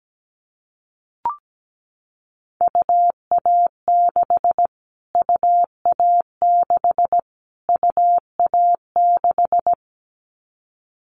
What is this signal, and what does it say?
Morse code, heard as a keyed tone, about 17 wpm, 700 Hz. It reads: UA6 UA6 UA6
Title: Callsign Prefixes for Top 25 DXCC - Repeated 3x - 17wpm